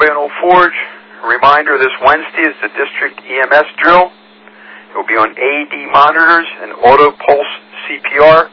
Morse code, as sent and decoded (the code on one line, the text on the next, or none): none